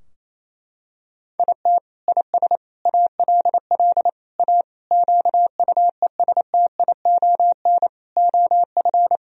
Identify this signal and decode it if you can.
Morse code, heard as a keyed tone, about 28 wpm, 700 Hz. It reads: IT IS ALL A QUESTION OF
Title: Sentences from Top 1000 Words - Review Part 3 - 28wpm